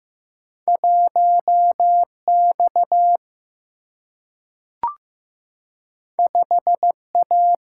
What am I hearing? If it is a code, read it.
Morse code, heard as a keyed tone, about 15 wpm, 700 Hz. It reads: 1X 5A
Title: Single-Letter Single-Number 15wpm